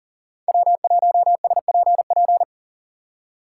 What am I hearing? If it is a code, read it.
Morse code, heard as a keyed tone, about 37 wpm, 700 Hz. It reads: W1SPP
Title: USA Call Signs - 5 Characters - 3x Character Spacing - 40wpm